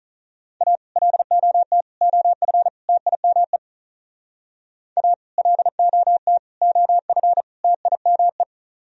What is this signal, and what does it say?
Morse code, heard as a keyed tone, about 38 wpm, 700 Hz. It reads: A LOT OF TIME A LOT OF TIME
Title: Speed-Racing - Sets of 4 Words 35wpm